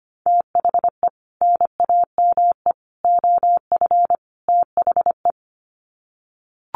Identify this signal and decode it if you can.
Morse code, heard as a keyed tone, about 25 wpm, 700 Hz. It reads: THE NAME OF THE
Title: Sets of 4 Words 25wpm